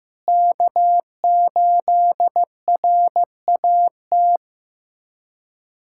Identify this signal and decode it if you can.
Morse code, heard as a keyed tone, about 16 wpm, 700 Hz. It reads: K8RAT
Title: USA Callsign DE Callsign 15wpm